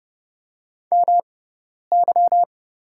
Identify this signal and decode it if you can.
Morse code, heard as a keyed tone, about 30 wpm, spaced as for 10 wpm, 700 Hz. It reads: MY
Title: 6x Character Spacing - Top 100 Words - 30wpm